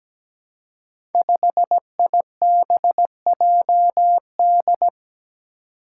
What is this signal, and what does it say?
Morse code, heard as a keyed tone, about 17 wpm, 700 Hz. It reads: 5IBJD
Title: Five Challenging Letter-Number Combinations 17wpm